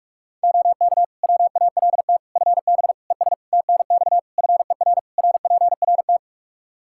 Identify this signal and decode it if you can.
Morse code, heard as a keyed tone, about 46 wpm, 700 Hz. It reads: OK WALT FB ES TNX FER RPRT
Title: Practice QSO - Exchange 2 - Info - 45wpm